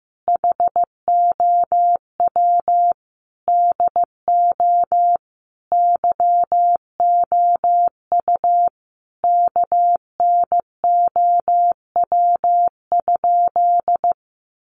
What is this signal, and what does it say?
Morse code, heard as a keyed tone, about 15 wpm, 700 Hz. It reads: HOW DO YOU KNOW?
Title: Sentences from Top 100 Words 15wpm